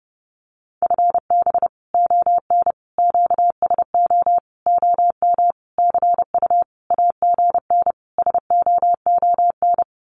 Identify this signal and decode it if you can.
Morse code, heard as a keyed tone, about 30 wpm, 700 Hz. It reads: FB ON QSO OM CU AGN SOON
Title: Common QSO Phrases 30wpm